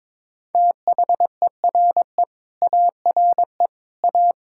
Morse code, heard as a keyed tone, about 22 wpm, 700 Hz. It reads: THERE ARE A
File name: practice / 22wpm / Sets of 7 Words - 22wpm